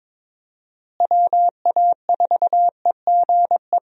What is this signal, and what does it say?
Morse code, heard as a keyed tone, about 22 wpm, 700 Hz. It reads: WA4EGE
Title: USA Call Signs - 6 Characters - Repeated 3x - 22wpm